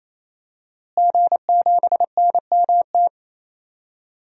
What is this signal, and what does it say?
Morse code, heard as a keyed tone, about 27 wpm, 700 Hz. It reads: G7NMT
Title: DXCC-100 Call Signs - 5 Characters - Repeated 3x - 28wpm